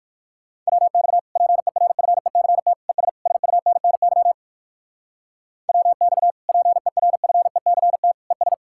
Morse code, heard as a keyed tone, about 59 wpm, 700 Hz. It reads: WX PERFECT ES SUNNY WX PERFECT ES